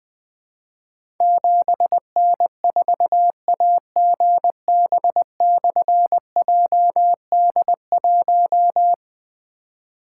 Morse code, heard as a keyed tone, about 20 wpm, 700 Hz. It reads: 7N4AGB/JD1
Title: Difficult Call Signs 20wpm